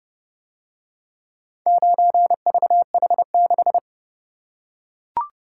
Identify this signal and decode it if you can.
Morse code, heard as a keyed tone, about 29 wpm, 700 Hz. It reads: 9VH6